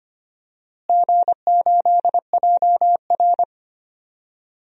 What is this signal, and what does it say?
Morse code, heard as a keyed tone, about 25 wpm, 700 Hz. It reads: G8JR
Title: DXCC-100 Call Signs - 4 Characters - Repeated 2x - 25wpm